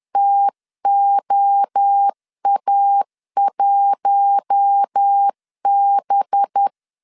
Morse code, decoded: TOA1B

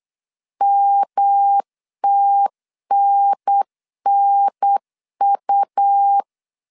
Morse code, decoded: MTNNU

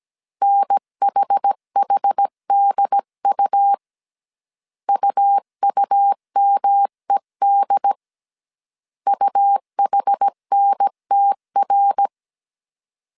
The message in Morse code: NHHDU UUMED UHNTR

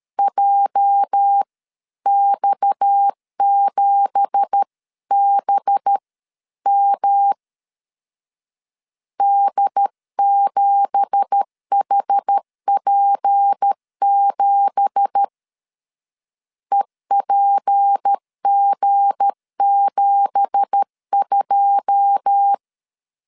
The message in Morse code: JX7BM D7HP7 EPG72

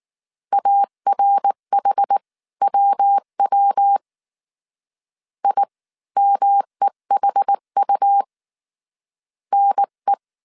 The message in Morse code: ARHWW IMEHU NE

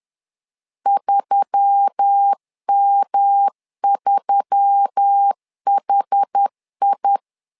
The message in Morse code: 3M3HI

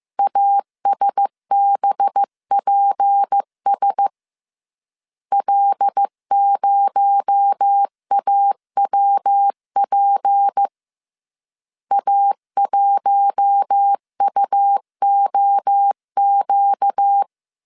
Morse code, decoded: ASBPS L0AWP A1UOQ

